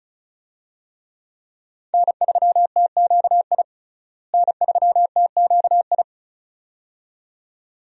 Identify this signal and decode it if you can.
Morse code, heard as a keyed tone, about 36 wpm, 700 Hz. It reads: N3TQI N3TQI
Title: USA Call Signs - 5 Characters - Repeated 2x - 35wpm